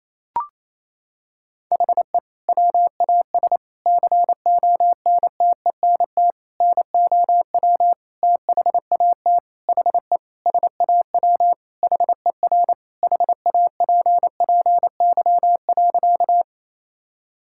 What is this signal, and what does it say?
Morse code, heard as a keyed tone, about 28 wpm, 700 Hz. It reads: HE WAS CONTENT NOW THAT HE SAW HER HAPPY.
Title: Sentences from Top 1000 Words - Review Part 6 - 28wpm